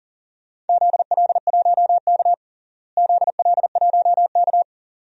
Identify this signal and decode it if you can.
Morse code, heard as a keyed tone, about 37 wpm, 700 Hz. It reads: ZL1K ZL1K
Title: DXCC-100 Call Signs - 4 Characters - Repeated 2x - 40wpm